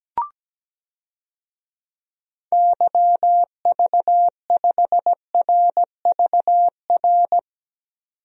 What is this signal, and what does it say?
Morse code, heard as a keyed tone, about 17 wpm, 700 Hz. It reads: YV5RVR
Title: DXCC-100 Call Signs - 6 Characters - 17wpm